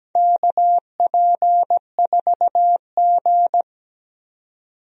Morse code, heard as a keyed tone, about 17 wpm, 700 Hz. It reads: KP4G
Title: DXCC-100 Call Signs - 4 Characters - 17wpm